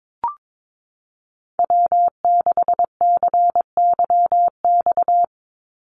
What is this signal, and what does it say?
Morse code, heard as a keyed tone, about 22 wpm, 700 Hz. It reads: W6CYX
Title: Call Signs Repeated 3x - 22wpm